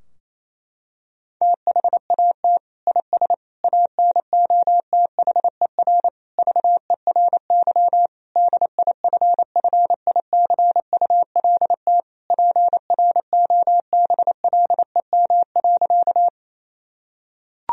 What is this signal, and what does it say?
Morse code, heard as a keyed tone, about 28 wpm, 700 Hz. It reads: THAT IS ANOTHER VERY DIFFICULT PROBLEM.
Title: Sentences from Top 1000 Words - Review Part 2 - 28wpm